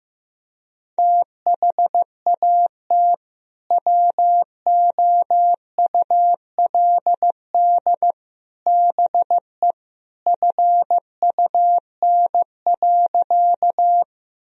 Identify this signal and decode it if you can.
Morse code, heard as a keyed tone, about 16 wpm, 700 Hz. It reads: THAT WOULD BE FUN.
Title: Sentences from Top 1000 Words - Review Part 4 - 15wpm